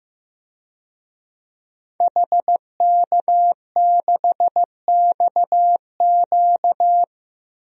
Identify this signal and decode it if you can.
Morse code, heard as a keyed tone, about 15 wpm, 700 Hz. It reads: HK6XQ
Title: DXCC-100 Call Signs - 5 Characters - 15wpm